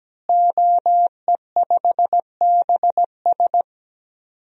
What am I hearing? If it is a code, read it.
Morse code, heard as a keyed tone, about 17 wpm, 700 Hz. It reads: OE5BS
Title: DXCC-100 Call Signs - 5 Characters - Repeated 3x - 17wpm